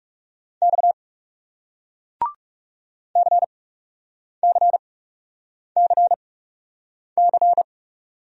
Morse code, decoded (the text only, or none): X CCCC